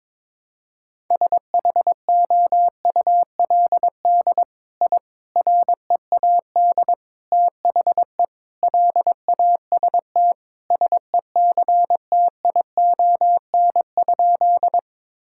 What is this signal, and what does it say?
Morse code, heard as a keyed tone, about 22 wpm, 700 Hz. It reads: SHOULD I READ THE LAST SECTION?
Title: Sentences from Top 1000 Words - Review Part 5 - 22wpm